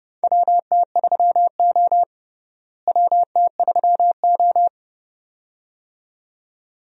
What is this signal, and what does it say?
Morse code, heard as a keyed tone, about 30 wpm, 700 Hz. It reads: WT3O WT3O
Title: USA Call Signs - 4 Characters - Repeated 2x - 30wpm